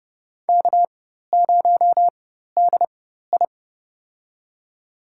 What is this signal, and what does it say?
Morse code, heard as a keyed tone, about 30 wpm, spaced as for 14 wpm, 700 Hz. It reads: K0DI
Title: USA Call Signs - 4 Characters - 4x Character Spacing - 30wpm